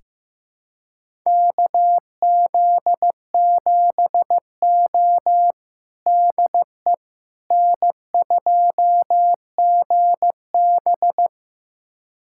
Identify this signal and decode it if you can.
Morse code, heard as a keyed tone, about 15 wpm, 700 Hz. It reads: KZ7O DE N2GB
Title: USA Callsign DE Callsign - 4 Characters - 15wpm